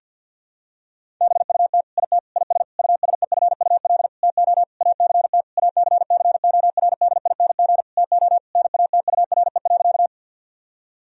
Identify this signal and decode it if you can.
Morse code, heard as a keyed tone, about 50 wpm, 700 Hz. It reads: BUT IT IS USEFUL TO ACT ACCORDING TO NATURE.
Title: Sentences from Top 1000 Words - Review Part 4 - 50wpm